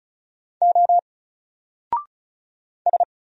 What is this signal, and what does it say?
Morse code, heard as a keyed tone, about 33 wpm, 700 Hz. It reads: O S
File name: Single-Letter-Number-35wpm